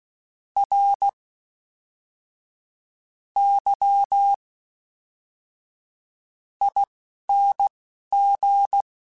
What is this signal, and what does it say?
Morse code, heard as a keyed tone, about 16 wpm, spaced as for 12 wpm, 800 Hz. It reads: R Y ING
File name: Special INGs / Special INGs-mp3s / Fast ing @10